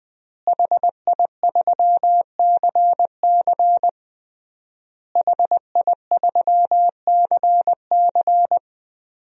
Morse code, heard as a keyed tone, about 20 wpm, 700 Hz. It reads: HI3CC HI3CC
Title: DXCC-100 Call Signs - 5 Characters - Repeated 3x - 20wpm